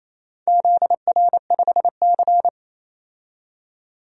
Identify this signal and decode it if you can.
Morse code, heard as a keyed tone, about 28 wpm, 700 Hz. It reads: ZR5C
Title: DXCC-100 Call Signs - 4 Characters - 28wpm